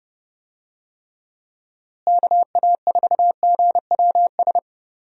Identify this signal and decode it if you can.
Morse code, heard as a keyed tone, about 30 wpm, 700 Hz. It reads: KA4GWS
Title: USA Call Signs - 6 Characters - Repeated 2x - 30wpm